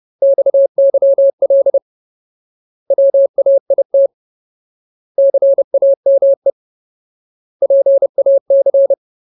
Morse code, heard as a keyed tone, about 30 wpm, 550 Hz. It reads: XYL WAIT CAME PAC
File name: Ultimate Word List / 30WPM Ultimate